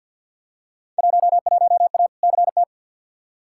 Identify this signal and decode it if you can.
Morse code, heard as a keyed tone, about 52 wpm, 700 Hz. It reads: 11A CT
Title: Field Day Exchanges Encore - 50wpm